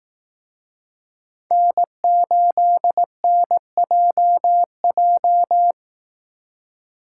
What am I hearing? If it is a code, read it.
Morse code, heard as a keyed tone, about 19 wpm, 700 Hz. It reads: N8NJJ
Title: Speed Racing - USA 1x3 Call Signs 15wpm